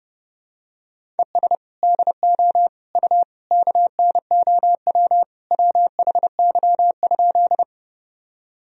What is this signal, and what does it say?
Morse code, heard as a keyed tone, about 30 wpm, 700 Hz. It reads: ES DO U KNOW WHY?
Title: Sentences from Top 200 Words and Abbreviations 30wpm